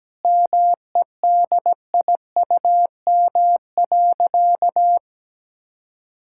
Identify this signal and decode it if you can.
Morse code, heard as a keyed tone, about 17 wpm, 700 Hz. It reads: MEDIUM.
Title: Sentences from Top 1000 Words - Review Part 6 - 17wpm